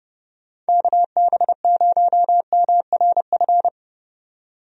Morse code, heard as a keyed tone, about 29 wpm, 700 Hz. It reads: KB0MRF
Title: USA Call Signs - 6 Characters - 30wpm